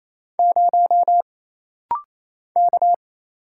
Morse code, heard as a keyed tone, about 27 wpm, 700 Hz. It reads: 0 K
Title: Single Letter-Number - Rapid-Fire 28wpm